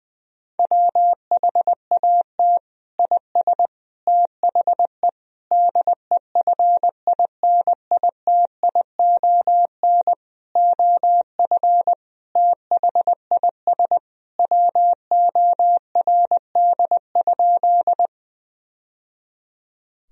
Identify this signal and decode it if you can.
Morse code, heard as a keyed tone, about 21 wpm, 700 Hz. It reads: WHAT IS THE DEFINITION OF THIS WORD?
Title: Sentences from Top 1000 Words - Review Part 4 - 20wpm